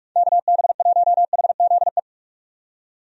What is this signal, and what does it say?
Morse code, heard as a keyed tone, about 46 wpm, 700 Hz. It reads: KB1HZE
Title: USA Call Signs - 6 Characters - 45wpm